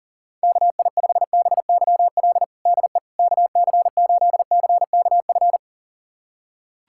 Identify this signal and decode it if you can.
Morse code, heard as a keyed tone, about 40 wpm, 700 Hz. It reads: KI5BYL DE KC8CKF
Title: USA Callsign DE Callsign - 6 Characters - 40wpm